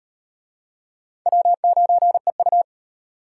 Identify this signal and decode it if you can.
Morse code, heard as a keyed tone, about 38 wpm, 700 Hz. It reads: W9EU